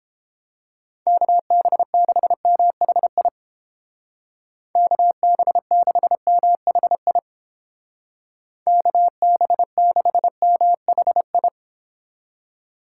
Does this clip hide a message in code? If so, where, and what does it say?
Yes: Morse code from 1.1 to 11.5 s, heard as a keyed tone, about 31 wpm, 700 Hz. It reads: KB6MHI KB6MHI KB6MHI